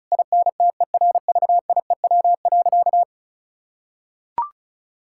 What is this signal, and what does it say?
Morse code, heard as a keyed tone, about 34 wpm, 700 Hz. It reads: INTERVIEW.
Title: Sentences from Top 1600 Words 35wpm